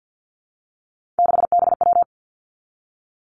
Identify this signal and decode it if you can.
Morse code, heard as a keyed tone, about 50 wpm, 700 Hz. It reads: 6BW
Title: Three Challenging Letter-Number Combinations 50wpm